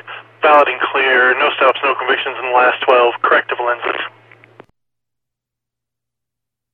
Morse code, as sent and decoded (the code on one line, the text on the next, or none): none